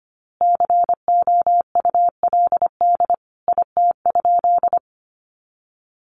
Morse code, heard as a keyed tone, about 25 wpm, 700 Hz. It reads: COULD IT?
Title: A Sound Of Thunder 25wpm